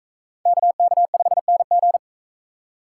Morse code, heard as a keyed tone, about 38 wpm, 700 Hz. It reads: KK5NG